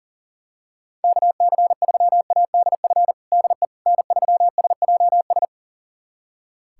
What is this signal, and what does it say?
Morse code, heard as a keyed tone, about 40 wpm, 700 Hz. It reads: KC3ADF DE N3SJS
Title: USA Callsign DE Callsign 40wpm